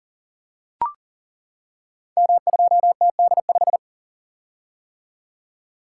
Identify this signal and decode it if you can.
Morse code, heard as a keyed tone, about 40 wpm, 700 Hz. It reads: M2TD5
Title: Code Groups - 5 Letter:Numbers - 40wpm